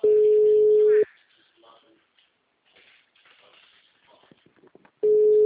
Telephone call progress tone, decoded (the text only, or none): ringback tone